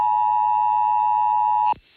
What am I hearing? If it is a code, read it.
Emergency Alert System (SAME) tones (the 853 and 960 Hz attention tone). It is heard from 0.0 to 1.8 s.